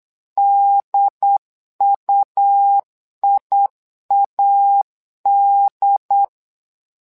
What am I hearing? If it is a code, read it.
Morse code, heard as a keyed tone, about 8 wpm, 800 Hz. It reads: DUIAD
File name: LEZIONI / 2 UHDN / 2 UHD 8wpm 2258